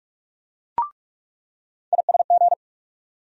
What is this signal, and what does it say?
Morse code, heard as a keyed tone, about 45 wpm, 700 Hz. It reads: ISG
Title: USA Call Sign Postfixes - 3 Letters 45wpm